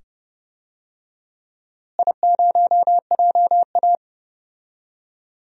Morse code, heard as a keyed tone, about 29 wpm, 700 Hz. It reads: I0JA
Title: DXCC-100 Call Signs - 4 Characters - Repeated 2x - 30wpm